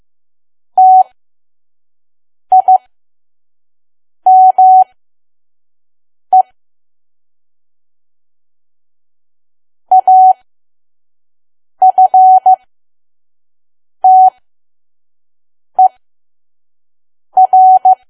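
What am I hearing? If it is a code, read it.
Morse code, heard as a keyed tone, about 15 wpm, spaced as for 5 wpm, 750 Hz. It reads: TIME AFTER
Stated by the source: Morse code practice at 5 WPM from KQ2P.COM.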